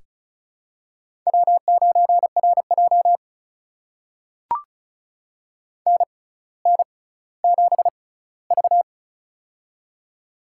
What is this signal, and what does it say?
Morse code, heard as a keyed tone, about 34 wpm, spaced as for 18 wpm, 700 Hz. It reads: W9RJ NN7V
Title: USA Call Signs - 4 Characters - 6x Character Spacing - 35wpm